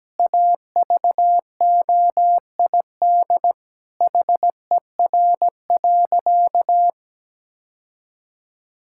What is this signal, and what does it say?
Morse code, heard as a keyed tone, about 17 wpm, 700 Hz. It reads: AVOID HER.
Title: Sentences from Top 1000 Words - Review Part 6 - 17wpm